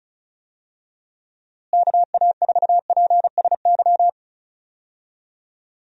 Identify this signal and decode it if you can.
Morse code, heard as a keyed tone, about 36 wpm, 700 Hz. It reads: KA4PSY
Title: USA Call Signs - 6 Characters - Repeated 2x - 35wpm